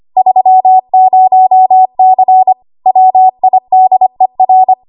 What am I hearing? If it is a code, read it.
Morse code, heard as a keyed tone, about 25 wpm, 750 Hz. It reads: 30C WIDER
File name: Morse / 25WPM